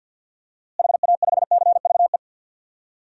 Morse code, heard as a keyed tone, about 49 wpm, 700 Hz. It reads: HA5CVE
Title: DXCC-100 Call Signs - 6 Characters - Repeated 2x - 50wpm